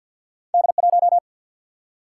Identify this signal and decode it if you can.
Morse code, heard as a keyed tone, about 53 wpm, 700 Hz. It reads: D1
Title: Single-Letter Single-Number 50wpm